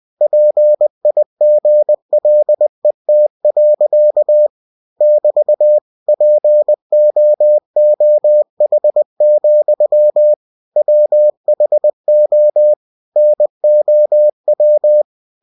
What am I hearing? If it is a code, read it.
Morse code, heard as a keyed tone, about 20 wpm, 600 Hz. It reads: PIGLET. <BT> POOH, WHO NOW
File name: winnie-20wpm